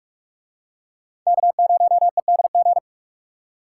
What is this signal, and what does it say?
Morse code, heard as a keyed tone, about 44 wpm, 700 Hz. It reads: K0EDG